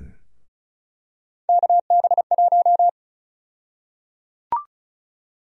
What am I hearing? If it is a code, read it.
Morse code, heard as a keyed tone, about 36 wpm, 700 Hz. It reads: KB1
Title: USA Call Sign Prefixes - Repeated 2x - 35wpm